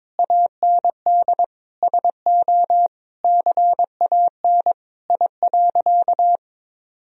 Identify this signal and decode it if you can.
Morse code, heard as a keyed tone, about 22 wpm, 700 Hz. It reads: AND SO CAN I.
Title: Sentences from Top 1000 Words - Review Part 2 - 22wpm